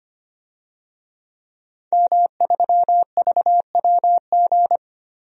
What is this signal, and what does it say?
Morse code, heard as a keyed tone, about 25 wpm, 700 Hz. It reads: M3VWG
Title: DXCC-100 Call Signs - 5 Characters - 25wpm